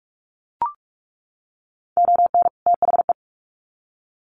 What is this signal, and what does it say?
Morse code, heard as a keyed tone, about 44 wpm, 700 Hz. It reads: ON THE
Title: Sets of 2 Words 45wpm